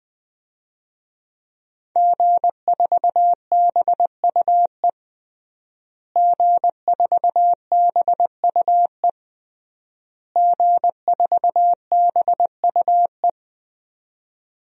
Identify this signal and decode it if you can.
Morse code, heard as a keyed tone, about 21 wpm, 700 Hz. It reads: G4BUE G4BUE G4BUE
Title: DXCC-100 Call Signs - 5 Characters - Repeated 3x - 20wpm